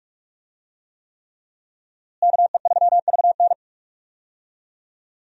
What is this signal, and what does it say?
Morse code, heard as a keyed tone, about 46 wpm, 700 Hz. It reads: KE3VN
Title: USA Call Signs - 5 Characters - Repeated 3x - 45wpm